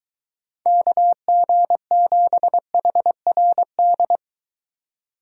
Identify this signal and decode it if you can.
Morse code, heard as a keyed tone, about 22 wpm, 700 Hz. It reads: KG7HRD